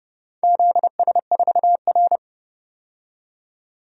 Morse code, heard as a keyed tone, about 31 wpm, 700 Hz. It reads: ZS4R
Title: DXCC-100 Call Signs - 4 Characters - 30wpm